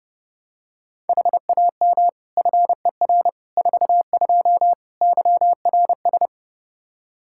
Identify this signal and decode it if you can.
Morse code, heard as a keyed tone, about 30 wpm, 700 Hz. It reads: HAM FER 42 YRS
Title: Practice QSO - Exchange 3 - More Info - 30wpm